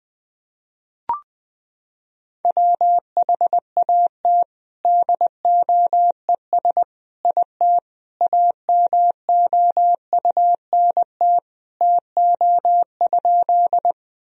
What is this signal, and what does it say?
Morse code, heard as a keyed tone, about 20 wpm, 700 Hz. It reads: WHAT DOES IT AMOUNT TO?
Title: Sentences from Top 1000 Words - Review Part 4 - 20wpm